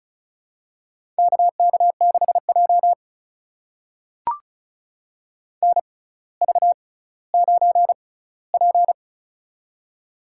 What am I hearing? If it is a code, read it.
Morse code, heard as a keyed tone, about 36 wpm, spaced as for 18 wpm, 700 Hz. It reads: KK6J NV9P
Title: USA Call Signs - 6x Character Spacing - 35wpm